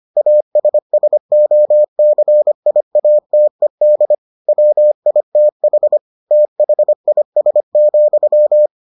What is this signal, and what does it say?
Morse code, heard as a keyed tone, about 25 wpm, 600 Hz. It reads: ASSOCIATED WITH THIS,